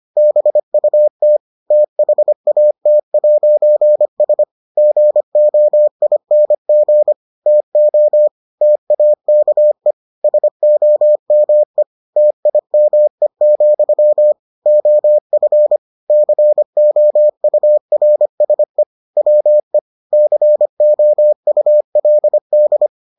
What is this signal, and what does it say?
Morse code, heard as a keyed tone, about 25 wpm, 600 Hz. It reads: BUT THAT'S GOING TO TAKE SOME TIME, OF COURSE WE COULD